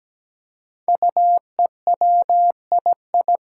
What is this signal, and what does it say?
Morse code, heard as a keyed tone, about 17 wpm, 700 Hz. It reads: UEWII
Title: Five Challenging Letter-Number Combinations 17wpm